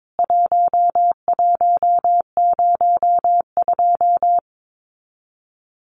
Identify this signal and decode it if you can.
Morse code, heard as a keyed tone, about 22 wpm, 700 Hz. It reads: 1102
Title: CWT Exchange 22wpm